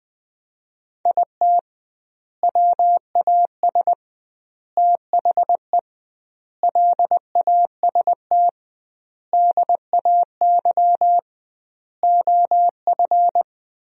Morse code, heard as a keyed tone, about 20 wpm, 700 Hz. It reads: IT WAS THE LAST DAY OF
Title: 2x Word-Spacing - Sentences from Top 300 Words - 20wpm